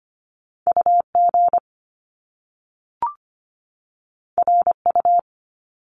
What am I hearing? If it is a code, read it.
Morse code, heard as a keyed tone, about 25 wpm, 700 Hz. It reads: UG RU
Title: Top Two Letter Combinations in Words 25wpm